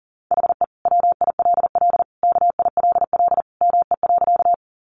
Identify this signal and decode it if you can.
Morse code, heard as a keyed tone, about 40 wpm, 700 Hz. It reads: HE WILL KILL ME.
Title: Sentences from Top 1400 Words 40wpm